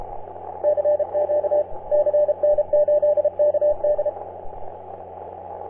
Morse code, heard as a keyed tone, about 30 wpm, 590 Hz, with radio noise. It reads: CQ CN8KD
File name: CN8KD on 10m CW